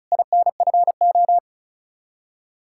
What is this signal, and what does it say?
Morse code, heard as a keyed tone, about 45 wpm, 700 Hz. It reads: INFO